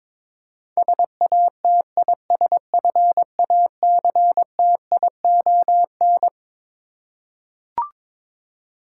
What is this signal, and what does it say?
Morse code, heard as a keyed tone, about 21 wpm, 700 Hz. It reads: SATISFACTION
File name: Twelve Letter Words - 22wpm